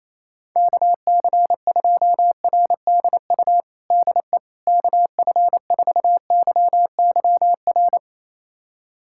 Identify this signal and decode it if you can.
Morse code, heard as a keyed tone, about 28 wpm, 700 Hz. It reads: KC2RDU DE KF4YYR
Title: USA Callsign DE Callsign 28wpm